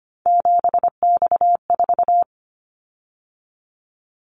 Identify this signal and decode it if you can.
Morse code, heard as a keyed tone, about 25 wpm, 700 Hz. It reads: 7X4
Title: Callsign Prefixes for Top 100 DXCC - 25wpm